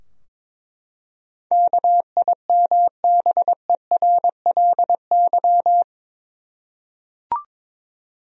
Morse code, heard as a keyed tone, about 22 wpm, 700 Hz. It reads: KIMBERLY
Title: USA Common Names - 22wpm